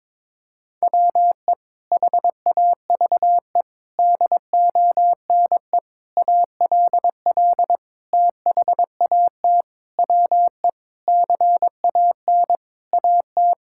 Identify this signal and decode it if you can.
Morse code, heard as a keyed tone, about 22 wpm, 700 Hz. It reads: WE HAVE DONE ALL THAT WE CAN AT
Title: Sentences from Top 1000 Words - Review Part 5 - 22wpm